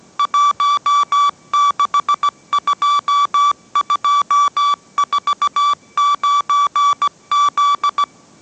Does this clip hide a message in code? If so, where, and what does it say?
Yes: Morse code from 0.2 to 8.0 s, heard as a keyed tone, about 17 wpm, 1150 Hz. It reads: 162249Z